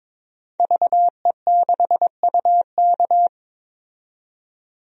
Morse code, heard as a keyed tone, about 22 wpm, 700 Hz. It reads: VE6UK
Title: DXCC-100 Call Signs - 5 Characters - 22wpm